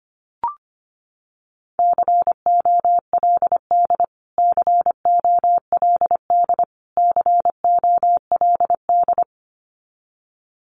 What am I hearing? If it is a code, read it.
Morse code, heard as a keyed tone, about 25 wpm, 700 Hz. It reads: COLD COLD COLD
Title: Top 500 Words Repeated 25wpm